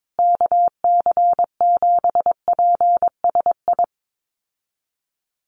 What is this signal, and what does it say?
Morse code, heard as a keyed tone, about 22 wpm, 700 Hz. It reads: KC7PSI
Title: USA Call Signs Repeated 3x 22wpm